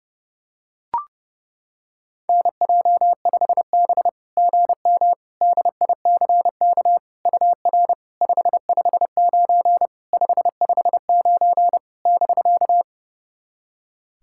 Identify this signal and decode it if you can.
Morse code, heard as a keyed tone, about 30 wpm, 700 Hz. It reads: NJ5B GM DICK UR 559 559 <BK>
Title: SOTA QSO Practice 30wpm